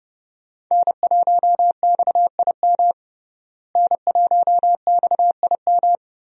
Morse code, heard as a keyed tone, about 32 wpm, 700 Hz. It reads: N1XIM N1XIM
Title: USA Call Signs - 5 Characters - Repeated 2x - 30wpm